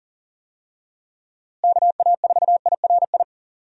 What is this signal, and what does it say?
Morse code, heard as a keyed tone, about 40 wpm, 700 Hz. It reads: KA4IRI
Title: USA Call Signs - 6 Characters - Repeated 2x - 40wpm